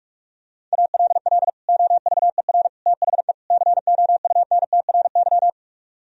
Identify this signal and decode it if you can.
Morse code, heard as a keyed tone, about 48 wpm, 700 Hz. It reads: ALL OVER THE COUNTRY